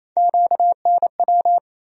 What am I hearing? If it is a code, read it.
Morse code, heard as a keyed tone, about 28 wpm, 700 Hz. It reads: QNW
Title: USA Call Sign Postfixes - 3 Letters 28wpm